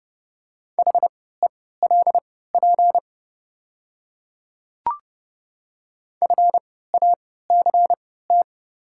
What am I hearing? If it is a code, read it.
Morse code, heard as a keyed tone, about 30 wpm, spaced as for 17 wpm, 700 Hz. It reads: HELP FACT
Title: 3x Character Spacing - Top 500 Words - 30wpm